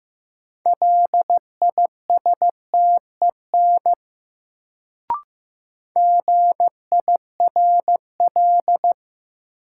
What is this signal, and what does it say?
Morse code, heard as a keyed tone, about 15 wpm, 700 Hz. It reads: LISTEN GIRL
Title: Sentences from Top 400 Words - New Word Drill - 15wpm